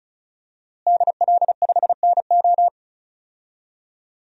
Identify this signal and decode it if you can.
Morse code, heard as a keyed tone, about 34 wpm, 700 Hz. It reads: DL5NO
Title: DXCC-100 Call Signs - 5 Characters - Repeated 3x - 35wpm